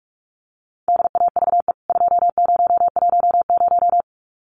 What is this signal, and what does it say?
Morse code, heard as a keyed tone, about 44 wpm, 700 Hz. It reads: DAVE 2010
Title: CWT Exchange 45wpm